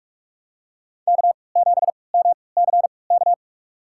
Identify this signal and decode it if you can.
Morse code, heard as a keyed tone, about 47 wpm, spaced as for 26 wpm, 700 Hz. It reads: K7MCK
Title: USA Call Signs - 3x Character Spacing - 45wpm